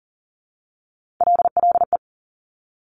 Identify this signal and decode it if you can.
Morse code, heard as a keyed tone, about 38 wpm, 700 Hz. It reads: LLE
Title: Top Three Letter Combinations in Words 40wpm